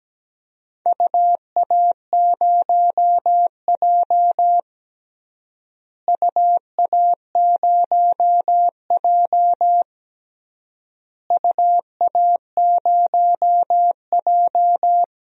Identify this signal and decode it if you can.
Morse code, heard as a keyed tone, about 17 wpm, 700 Hz. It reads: UA0J UA0J UA0J
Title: DXCC-100 Call Signs - 4 Characters - Repeated 3x - 17wpm